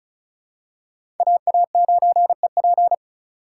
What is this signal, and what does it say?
Morse code, heard as a keyed tone, about 36 wpm, 700 Hz. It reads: AA9EP
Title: USA Call Signs - 5 Characters - 4x Character Spacing - 35wpm